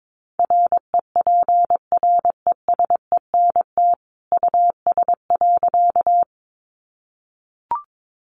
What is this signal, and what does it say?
Morse code, heard as a keyed tone, about 22 wpm, 700 Hz. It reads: REPRESENT US.
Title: Sentences from Top 1200 Words 22wpm